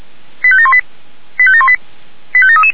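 Five-tone selective call (ZVEI), heard as quick runs of five tones, tones of about 70 ms. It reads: 85718 85718 85733
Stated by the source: ZVEI five-tone selective call received by SDR.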